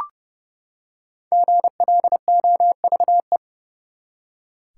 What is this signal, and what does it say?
Morse code, heard as a keyed tone, about 29 wpm, 700 Hz. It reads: GLOVE
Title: Five Letter Words - 30wpm